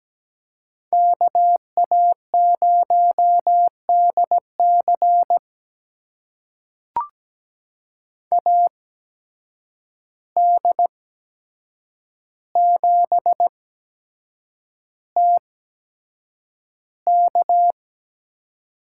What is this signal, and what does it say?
Morse code, heard as a keyed tone, about 17 wpm, spaced as for 7 wpm, 700 Hz. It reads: KA0DC AD7TK
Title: USA Call Signs - 5 Characters - 8x Character Spacing - 17wpm